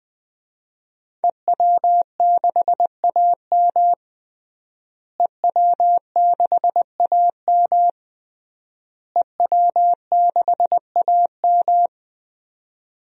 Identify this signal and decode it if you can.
Morse code, heard as a keyed tone, about 21 wpm, 700 Hz. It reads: EW6AM EW6AM EW6AM
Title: DXCC-100 Call Signs - 5 Characters - Repeated 3x - 20wpm